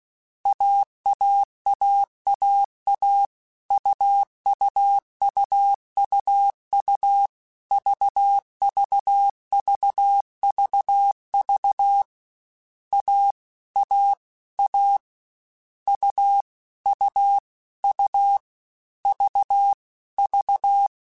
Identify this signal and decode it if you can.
Morse code, heard as a keyed tone, about 16 wpm, 800 Hz. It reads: AAAAA UUUUU VVVVV A A A U U U V V